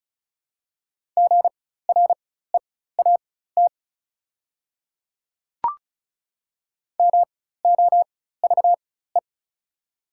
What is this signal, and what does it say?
Morse code, heard as a keyed tone, about 36 wpm, spaced as for 16 wpm, 690 Hz. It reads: GREAT MOVE
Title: 4x Character Spacing - Top 200 Words - 35wpm